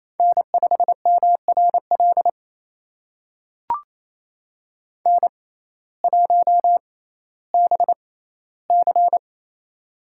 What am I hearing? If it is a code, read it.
Morse code, heard as a keyed tone, about 28 wpm, spaced as for 15 wpm, 700 Hz. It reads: N5MRL N1BC
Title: USA Call Signs - 6x Character Spacing - 28wpm